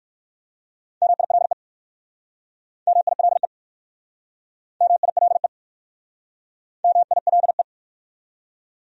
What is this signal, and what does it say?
Morse code, heard as a keyed tone, about 59 wpm, 700 Hz. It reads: MILE MILE MILE MILE